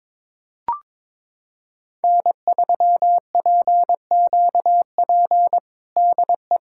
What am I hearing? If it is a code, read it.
Morse code, heard as a keyed tone, about 22 wpm, 700 Hz. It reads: N3PQP DE
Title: USA Callsign DE Callsign - 5 Characters - 22wpm